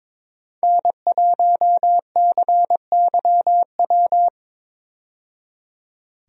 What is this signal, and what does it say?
Morse code, heard as a keyed tone, about 22 wpm, 700 Hz. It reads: N1CYW
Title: USA Call Signs - 5 Characters - 22wpm